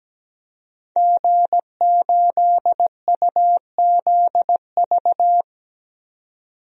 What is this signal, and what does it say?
Morse code, heard as a keyed tone, about 17 wpm, 700 Hz. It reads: G8UZV
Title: DXCC-100 Call Signs - 5 Characters - 17wpm